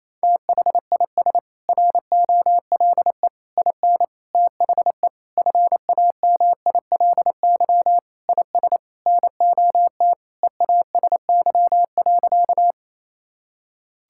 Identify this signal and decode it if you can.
Morse code, heard as a keyed tone, about 27 wpm, 700 Hz. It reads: THIS ROLE IN THE FAMILY IS NOT EASY.
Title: Sentences from Top 1000 Words - Review Part 6 - 28wpm